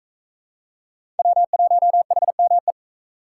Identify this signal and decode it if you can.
Morse code, heard as a keyed tone, about 40 wpm, 700 Hz. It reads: W1HME